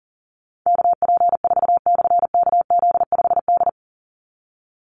Morse code, heard as a keyed tone, about 40 wpm, 700 Hz. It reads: KP4/KZ5D
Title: Call Signs Repeated 2x - 40wpm